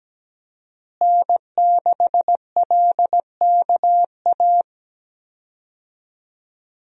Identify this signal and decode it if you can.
Morse code, heard as a keyed tone, about 17 wpm, 700 Hz. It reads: N6LKA
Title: USA Call Signs - 5 Characters - Repeated 3x - 17wpm